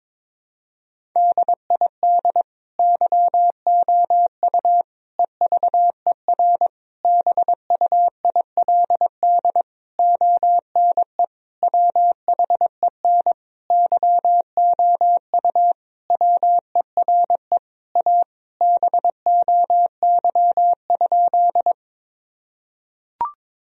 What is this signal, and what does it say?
Morse code, heard as a keyed tone, about 22 wpm, 700 Hz. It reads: DID YOU EVER BUILD ONE WHEN YOU WERE A BOY?
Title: Sentences from Top 1000 Words - Review Part 1 - 22wpm